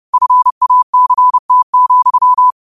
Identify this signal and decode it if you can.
Morse code, heard as a keyed tone, about 31 wpm, 1000 Hz. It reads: FACT,
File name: qotdfiles / 30wpm